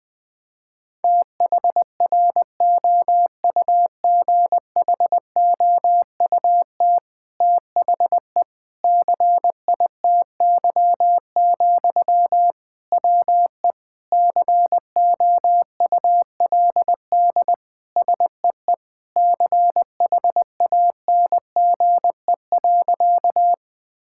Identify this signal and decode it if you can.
Morse code, heard as a keyed tone, about 20 wpm, 700 Hz. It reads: THROUGHOUT THE CITY, WE COULD SEE CHANGE.
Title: Sentences from Top 1000 Words - Review Part 5 - 20wpm